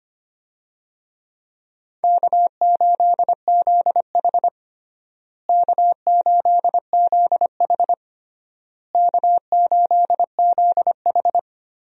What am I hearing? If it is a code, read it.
Morse code, heard as a keyed tone, about 25 wpm, 700 Hz. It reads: K8ZH K8ZH K8ZH
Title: USA Call Signs - 4 Characters - Repeated 3x - 25wpm